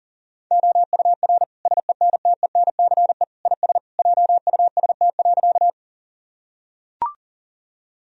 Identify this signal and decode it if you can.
Morse code, heard as a keyed tone, about 40 wpm, 700 Hz. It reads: OUR SENTENCE IS JUST.
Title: Sentences from Top 1000 Words - Review Part 1 - 40wpm